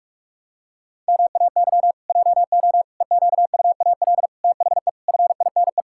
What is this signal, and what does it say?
Morse code, heard as a keyed tone, about 46 wpm, 700 Hz. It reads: MAY JO EQUAL THE FINE
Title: Pangram Challenge v2 - 45wpm